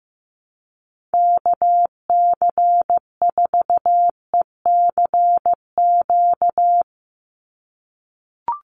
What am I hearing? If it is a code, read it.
Morse code, heard as a keyed tone, about 15 wpm, 700 Hz. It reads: KC4ECQ
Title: USA Call Signs 15wpm